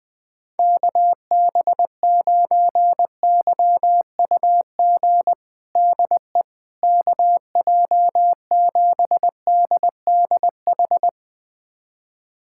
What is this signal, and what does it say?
Morse code, heard as a keyed tone, about 21 wpm, 700 Hz. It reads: KB9YUG DE KJ7DDH
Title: USA Callsign DE Callsign - 6 Characters - 20wpm